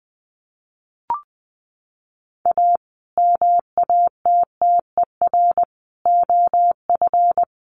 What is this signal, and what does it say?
Morse code, heard as a keyed tone, about 20 wpm, 700 Hz. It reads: A MATTER OF
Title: Sets of 5 Words Encore 20wpm